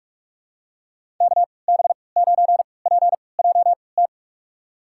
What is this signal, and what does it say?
Morse code, heard as a keyed tone, about 43 wpm, spaced as for 25 wpm, 700 Hz. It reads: KB9PJT
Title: USA Call Signs - 3x Character Spacing - 45wpm